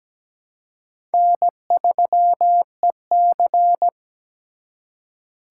Morse code, heard as a keyed tone, about 20 wpm, 700 Hz. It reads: N3EC
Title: Speed Racing - USA 1x2 Call Signs 17wpm